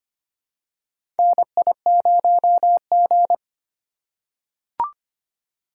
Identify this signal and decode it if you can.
Morse code, heard as a keyed tone, about 25 wpm, 700 Hz. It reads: NI0G
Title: USA Call Signs - 4 Characters - 2x Character Spacing - 25wpm